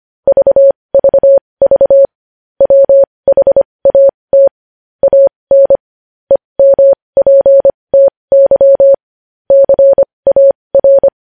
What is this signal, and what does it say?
Morse code, heard as a keyed tone, about 25 wpm, spaced as for 20 wpm, 550 Hz. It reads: VVV WHAT AN EMPTY CAR
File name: podcast / News headlines-1x-25x20